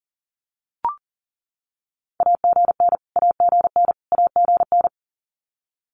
Morse code, heard as a keyed tone, about 40 wpm, 700 Hz. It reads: AGN AGN AGN
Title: Abbreviations Repeated 3x - 40wpm